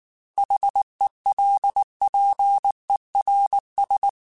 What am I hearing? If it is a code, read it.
Morse code, heard as a keyed tone, about 19 wpm, 800 Hz. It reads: HELPERS